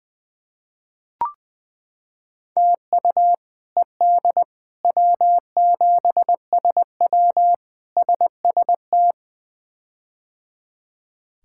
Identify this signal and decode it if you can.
Morse code, heard as a keyed tone, about 20 wpm, 700 Hz. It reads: TU ED W7SW SST
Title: K1USN SST QSO Practice - 20wpm